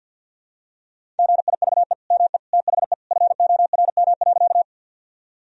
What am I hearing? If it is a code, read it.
Morse code, heard as a keyed tone, about 50 wpm, 700 Hz. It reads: GIVE ME THE FORM.
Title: Sentences from Top 1000 Words - Review Part 1 - 50wpm